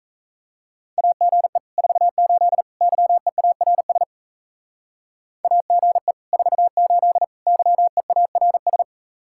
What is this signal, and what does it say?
Morse code, heard as a keyed tone, about 36 wpm, 700 Hz. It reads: AGE 48 YEARS AGE 48 YEARS